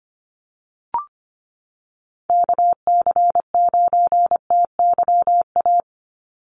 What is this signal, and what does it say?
Morse code, heard as a keyed tone, about 25 wpm, 700 Hz. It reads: KC9TYA
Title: NA SOTA Chaser Callsigns 25wpm